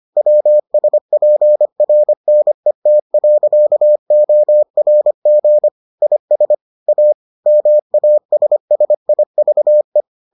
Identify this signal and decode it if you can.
Morse code, heard as a keyed tone, about 25 wpm, 600 Hz. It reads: WSPRNET.ORG IS A MASSIVE